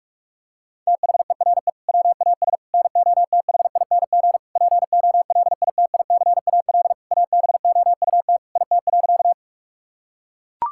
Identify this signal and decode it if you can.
Morse code, heard as a keyed tone, about 41 wpm, 700 Hz. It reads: THERE WAS NOTHING POLITICAL ABOUT IT.
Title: Sentences from Top 1000 Words - Review Part 5 - 45wpm